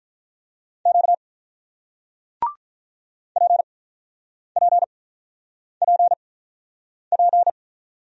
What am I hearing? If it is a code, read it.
Morse code, heard as a keyed tone, about 48 wpm, 700 Hz. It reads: Q P P P P